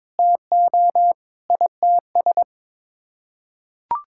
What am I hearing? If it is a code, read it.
Morse code, heard as a keyed tone, about 21 wpm, 700 Hz. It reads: TO ITS
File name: Sets of 9 Words - 22wpm